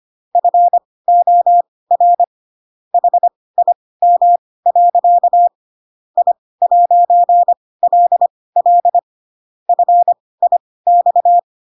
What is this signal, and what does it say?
Morse code, heard as a keyed tone, about 25 wpm, spaced as for 18 wpm, 700 Hz. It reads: FOR HIM. I'LL FIX
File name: Pinocchio / Pinocchio ch1 12 25 ik1hge